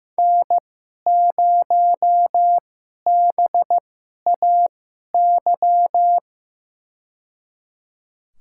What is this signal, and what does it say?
Morse code, heard as a keyed tone, about 15 wpm, spaced as for 11 wpm, 700 Hz. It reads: N0BAY
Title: USA Call Signs - 5 Characters - 2x Character Spacing - 15wpm